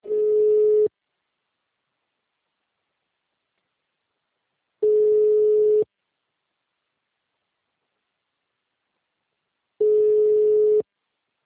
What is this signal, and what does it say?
Ringback tone (Europe, 425 Hz).